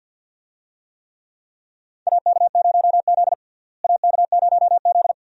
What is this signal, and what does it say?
Morse code, heard as a keyed tone, about 51 wpm, 700 Hz. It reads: AK0Z AK0Z
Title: USA Call Signs - 4 Characters - Repeated 2x - 50wpm